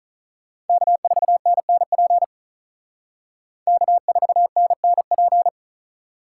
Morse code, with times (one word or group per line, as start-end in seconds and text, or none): 0.7-2.2 K4NNP
3.7-5.5 K4NNP